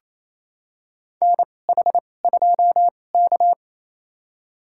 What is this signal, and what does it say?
Morse code, heard as a keyed tone, about 28 wpm, spaced as for 20 wpm, 700 Hz. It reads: NH2K
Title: USA Call Signs - 2x Character Spacing - 28wpm